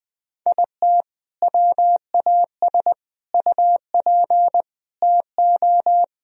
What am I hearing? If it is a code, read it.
Morse code, heard as a keyed tone, about 21 wpm, 700 Hz. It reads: IT WAS UP TO
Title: Sentences from Top 1000 Words - Review Part 2 - 20wpm